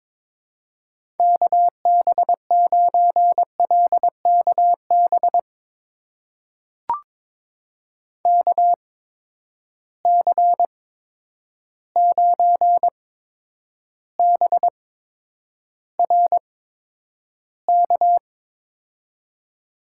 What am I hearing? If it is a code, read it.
Morse code, heard as a keyed tone, about 22 wpm, spaced as for 9 wpm, 700 Hz. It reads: KB9LKB KC9BRK
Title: USA Call Signs - 6 Characters - 8x Character Spacing - 22wpm